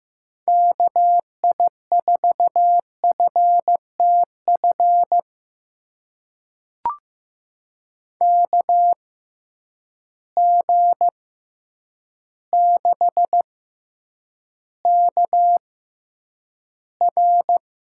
Morse code, heard as a keyed tone, about 15 wpm, spaced as for 8 wpm, 700 Hz. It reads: KI4FTF KG6KR
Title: USA Call Signs - 6 Characters - 6x Character Spacing - 15wpm